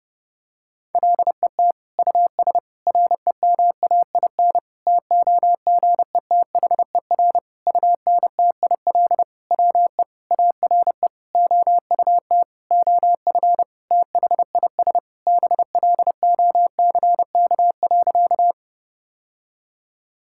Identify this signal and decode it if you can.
Morse code, heard as a keyed tone, about 30 wpm, 700 Hz. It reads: LET US REMAIN TOGETHER UNTIL WE ARE OUT OF THIS BLOCK.
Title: Sentences from Top 1600 Words 30wpm